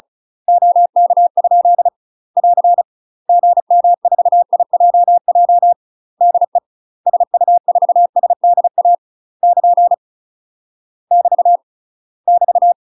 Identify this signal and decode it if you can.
Morse code, heard as a keyed tone, about 36 wpm, 700 Hz. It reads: OK? <AR> GM4IJJ DE SU4SDA <KN> <BT> <BT>
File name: UK MASTER QSOs TRAINING / 35WPM / UK MASTER QSOs 0013